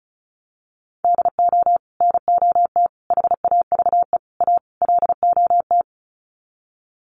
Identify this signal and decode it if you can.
Morse code, heard as a keyed tone, about 34 wpm, 700 Hz. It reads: DO NOT HAVE A LOT
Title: Sets of 5 Words Encore 35wpm